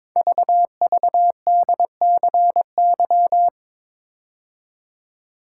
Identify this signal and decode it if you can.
Morse code, heard as a keyed tone, about 22 wpm, 700 Hz. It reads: VVDCY
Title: Code Groups - 5 Letters - 22wpm